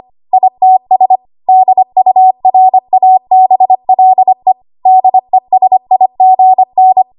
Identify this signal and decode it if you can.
Morse code, heard as a keyed tone, about 25 wpm, 750 Hz. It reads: ITS DURABLE DESIGN
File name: Archive / 25 WPM / 25WPM